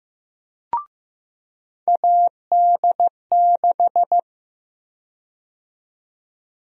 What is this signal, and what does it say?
Morse code, heard as a keyed tone, about 15 wpm, 700 Hz. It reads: AD6
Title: USA Call Sign Prefixes - 15wpm